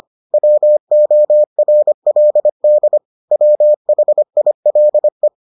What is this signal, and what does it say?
Morse code, heard as a keyed tone, about 25 wpm, 600 Hz. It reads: WORLD WHILE